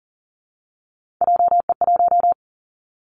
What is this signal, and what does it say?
Morse code, heard as a keyed tone, about 40 wpm, 700 Hz. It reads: JE1
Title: Callsign Prefixes for Top 25 DXCC - Repeated 3x - 40wpm